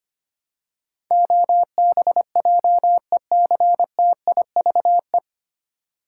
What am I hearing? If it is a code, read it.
Morse code, heard as a keyed tone, about 25 wpm, 700 Hz. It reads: OBJECTIVE
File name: Nine Letter Words - 25wpm